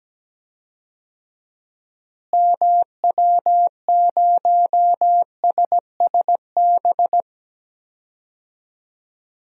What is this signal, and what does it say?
Morse code, heard as a keyed tone, about 17 wpm, 700 Hz. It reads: MW0SSB
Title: DXCC-100 Call Signs - 6 Characters - Repeated 3x - 17wpm